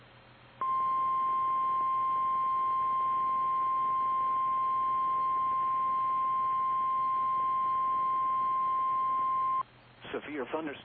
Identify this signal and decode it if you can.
Emergency Alert System (SAME) tones (the 1050 Hz attention tone), with speech. It is heard from 0.5 to 9.7 s.
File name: eas / KKPRKXPN / KKPRKXPN